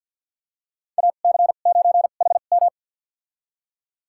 Morse code, heard as a keyed tone, about 47 wpm, spaced as for 35 wpm, 690 Hz. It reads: AC9HM
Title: USA Call Signs - 2x Character Spacing - 50wpm